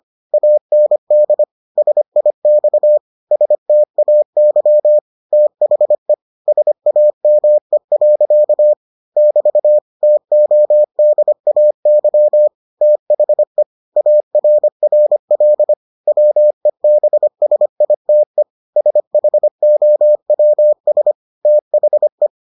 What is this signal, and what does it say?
Morse code, heard as a keyed tone, about 25 wpm, 600 Hz. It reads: AND SIX STAY THE SAME. <BT> TODAY THE ARRL WEBSITE SHOWS THE